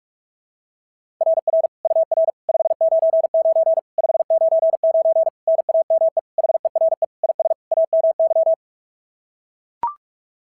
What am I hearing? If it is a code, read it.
Morse code, heard as a keyed tone, about 42 wpm, 660 Hz. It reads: RR UR 599 599 NAME HERE IS AMY
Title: SOTA QSO Practice 45wpm